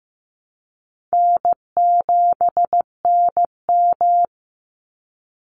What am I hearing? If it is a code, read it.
Morse code, heard as a keyed tone, about 15 wpm, 700 Hz. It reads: N7NM
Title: Easy Call Signs - 15wpm